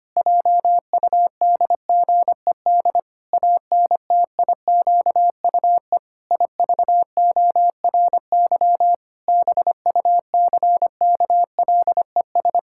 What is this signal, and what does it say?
Morse code, heard as a keyed tone, about 25 wpm, 700 Hz. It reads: JUDGED ANTIQUE IVORY BUCKLES
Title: Pangram Challenge v2 - 25wpm